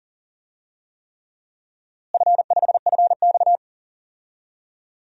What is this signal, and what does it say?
Morse code, heard as a keyed tone, about 40 wpm, 700 Hz. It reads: F5FX
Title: DXCC-100 Call Signs - 4 Characters - Repeated 3x - 40wpm